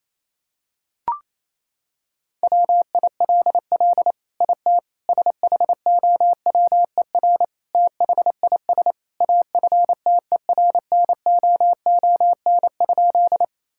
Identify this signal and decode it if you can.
Morse code, heard as a keyed tone, about 28 wpm, 700 Hz. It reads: WILL IT SHOWER THIS AFTERNOON?
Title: Sentences from Top 1600 Words 28wpm